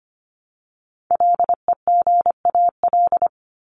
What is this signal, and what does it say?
Morse code, heard as a keyed tone, about 25 wpm, 700 Hz. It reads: LEGAL
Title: Top 1001-1250 Words 25wpm